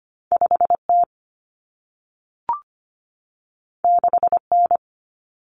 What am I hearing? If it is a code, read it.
Morse code, heard as a keyed tone, about 25 wpm, 700 Hz. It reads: 5T 6N
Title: Common Cut Numbers - 2-Digit 25wpm